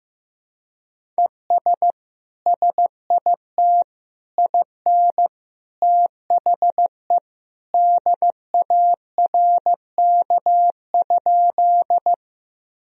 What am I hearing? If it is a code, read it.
Morse code, heard as a keyed tone, about 15 wpm, 700 Hz. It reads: ES SIT IN THE DARK?
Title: Sentences from Top 500 Words and Abbreviations 15wpm